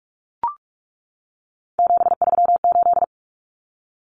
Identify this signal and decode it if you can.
Morse code, heard as a keyed tone, about 44 wpm, 700 Hz. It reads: 738
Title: Common Cut Numbers - 3-Digit 45wpm